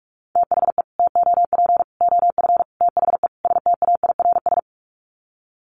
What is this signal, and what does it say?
Morse code, heard as a keyed tone, about 44 wpm, 700 Hz. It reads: THE TOP OF THE STAIRS
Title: Sets of 5 Words Encore 45wpm